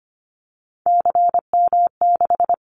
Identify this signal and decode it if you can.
Morse code, heard as a keyed tone, about 25 wpm, 700 Hz. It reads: CM6
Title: Callsign Prefixes for Top 100 DXCC - 25wpm